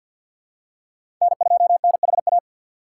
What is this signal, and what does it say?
Morse code, heard as a keyed tone, about 47 wpm, 700 Hz. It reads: N2NHA
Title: USA Call Signs - 5 Characters - 3x Character Spacing - 50wpm